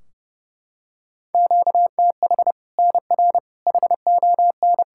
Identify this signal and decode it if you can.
Morse code, heard as a keyed tone, about 30 wpm, 700 Hz. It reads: QTH NR HON
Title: Practice QSO - Exchange 1 - Beginning - 30wpm